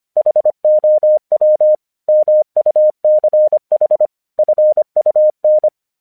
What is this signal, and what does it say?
Morse code, heard as a keyed tone, about 25 wpm, 610 Hz. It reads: HOW MUCH FUN
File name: 1 English 25wpm / AQC32